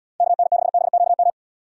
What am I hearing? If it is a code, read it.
Morse code, heard as a keyed tone, about 69 wpm, 700 Hz. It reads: KN6BQM